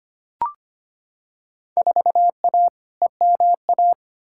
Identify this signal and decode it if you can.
Morse code, heard as a keyed tone, about 25 wpm, 700 Hz. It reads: 4A EMA
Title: Field Day Exchanges - 25wpm